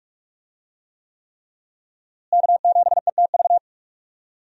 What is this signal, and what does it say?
Morse code, heard as a keyed tone, about 43 wpm, 700 Hz. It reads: K7ETV
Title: USA Call Signs - 5 Characters - Repeated 3x - 45wpm